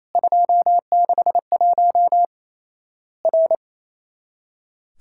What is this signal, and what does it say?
Morse code, heard as a keyed tone, about 28 wpm, 700 Hz. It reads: 261 R
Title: Worked All Europe DX Contest - QTC Exchange - 28wpm